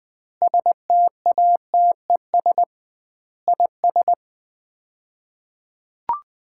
Morse code, heard as a keyed tone, about 21 wpm, 700 Hz. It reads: STATES IS
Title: 2x Word-Spacing - Sets of 5 Words - 20wpm